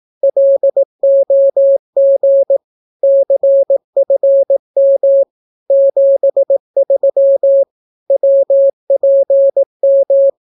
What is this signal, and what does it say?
Morse code, heard as a keyed tone, about 18 wpm, 550 Hz. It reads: LOG CFM 73 WPM
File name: naukapodstaw / skroty / skroty